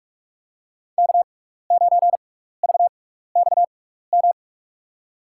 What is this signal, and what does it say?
Morse code, heard as a keyed tone, about 43 wpm, spaced as for 15 wpm, 700 Hz. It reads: K9VXM
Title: USA Call Signs - 5 Characters - 6x Character Spacing - 45wpm